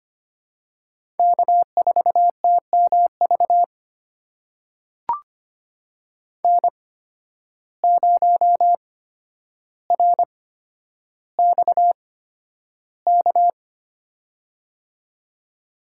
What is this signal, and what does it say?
Morse code, heard as a keyed tone, about 25 wpm, spaced as for 11 wpm, 700 Hz. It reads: K4TMV N0RXK
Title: USA Call Signs - 5 Characters - 8x Character Spacing - 25wpm